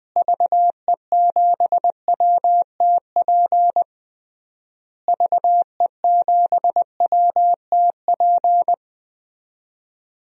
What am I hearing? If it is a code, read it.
Morse code, heard as a keyed tone, about 20 wpm, 700 Hz. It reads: VE7WTP VE7WTP
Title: DXCC-100 Call Signs - 6 Characters - Repeated 3x - 20wpm